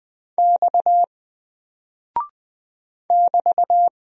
Morse code, heard as a keyed tone, about 20 wpm, 700 Hz. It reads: X <BT>
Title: Single Character (Letter:Number:Punctuation:Prosign) 20wpm